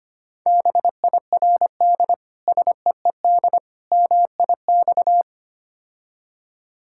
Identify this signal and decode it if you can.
Morse code, heard as a keyed tone, about 25 wpm, 700 Hz. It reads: BIRD SEED MIX
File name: practice / 25wpm / Difficult Words with Hints - 25wpm